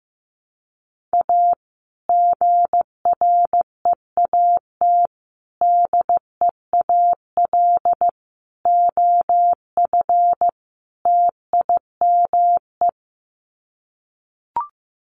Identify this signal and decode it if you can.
Morse code, heard as a keyed tone, about 15 wpm, 700 Hz. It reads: A GREAT DEAL OF TIME
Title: Sets of 5 Words Encore2 15wpm